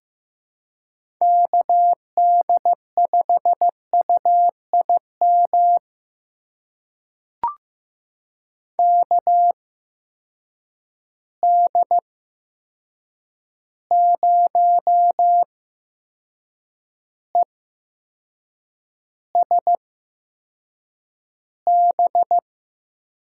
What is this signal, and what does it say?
Morse code, heard as a keyed tone, about 15 wpm, spaced as for 6 wpm, 700 Hz. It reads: KD5UIM KD0ESB
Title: USA Call Signs - 6 Characters - 8x Character Spacing - 15wpm